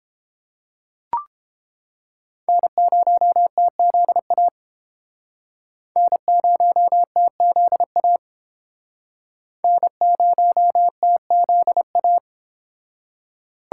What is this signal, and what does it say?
Morse code, heard as a keyed tone, about 29 wpm, 700 Hz. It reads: N0TZA N0TZA N0TZA